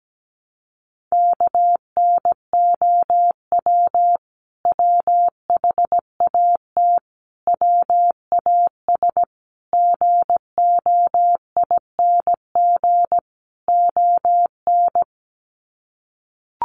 Morse code, heard as a keyed tone, about 17 wpm, 700 Hz. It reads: KNOW WHAT WAS GOING ON
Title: Sets of 5 Words Encore2 17wpm